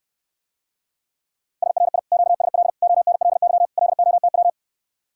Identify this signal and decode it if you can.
Morse code, heard as a keyed tone, about 69 wpm, 700 Hz. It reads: SRI QSL ONLY LOTW